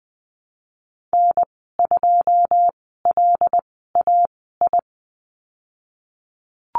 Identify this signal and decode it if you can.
Morse code, heard as a keyed tone, about 20 wpm, spaced as for 15 wpm, 700 Hz. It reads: N2LAI
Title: 122 Lesson 20g - Number 3 - Call Signs 20wpm